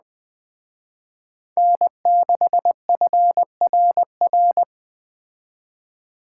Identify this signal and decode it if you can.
Morse code, heard as a keyed tone, about 20 wpm, 700 Hz. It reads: N6FRR
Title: Speed Racing - USA 1x3 Call Signs 17wpm